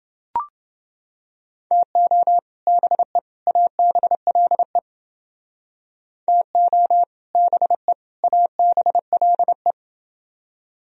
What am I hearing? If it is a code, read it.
Morse code, heard as a keyed tone, about 29 wpm, 700 Hz. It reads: TO BE ABLE TO BE ABLE